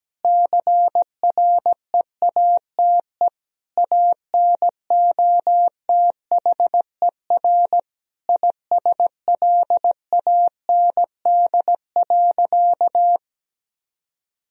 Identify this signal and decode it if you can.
Morse code, heard as a keyed tone, about 17 wpm, 700 Hz. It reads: CREATE ANOTHER ISLAND.
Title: Sentences from Top 1000 Words - Review Part 3 - 17wpm